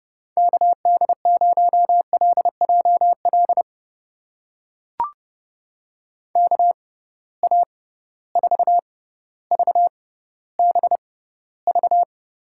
Morse code, heard as a keyed tone, about 30 wpm, spaced as for 15 wpm, 700 Hz. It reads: KD0LJL KA4VBV
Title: USA Call Signs - 6 Characters - 6x Character Spacing - 30wpm